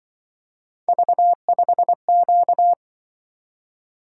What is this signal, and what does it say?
Morse code, heard as a keyed tone, about 24 wpm, 700 Hz. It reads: V5Q
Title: Speed-Racing - Three Letter-Number Combinations - 20wpm